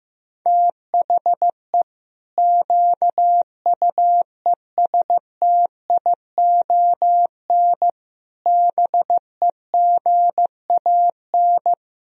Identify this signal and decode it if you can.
Morse code, heard as a keyed tone, about 15 wpm, 700 Hz. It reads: THE QUESTION BEGAN
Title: Sentences from Top 1000 Words - Review Part 3 - 15wpm